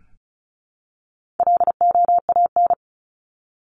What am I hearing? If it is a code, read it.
Morse code, heard as a keyed tone, about 34 wpm, 700 Hz. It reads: LOAN
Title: Top 1501-1750 Words 35wpm